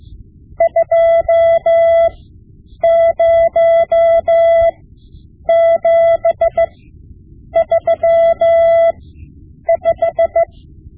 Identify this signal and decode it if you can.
Morse code, heard as a keyed tone, about 16 wpm, spaced as for 8 wpm, 670 Hz. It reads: 20735
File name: Morse Lessons / Morse Lessons / Tape-5B